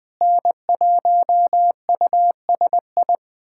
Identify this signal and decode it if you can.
Morse code, heard as a keyed tone, about 20 wpm, 700 Hz. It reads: N1USI
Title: USA Callsign DE Callsign - 5 Characters - 20wpm